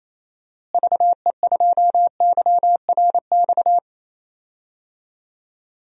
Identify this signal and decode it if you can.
Morse code, heard as a keyed tone, about 28 wpm, 700 Hz. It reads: VE2YRX
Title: DXCC-100 Call Signs - 6 Characters - Repeated 3x - 28wpm